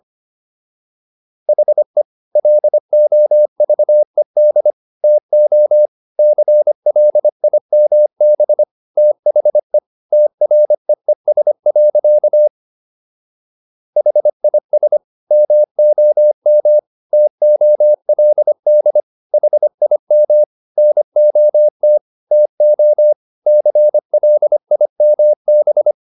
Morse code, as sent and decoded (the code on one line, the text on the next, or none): .... / .-.. --- ...- . -.. / - --- / -.-. .-.. .. -- -... / - .... / - .-. . ... .-.-.- / .... .. ... / -- --- -- / - --- .-.. -.. / .... .. -- / -. --- - / - --- / -.-. .-.. .. -- -...
HE LOVED TO CLIMB THE TREES. HIS MOM TOLD HIM NOT TO CLIMB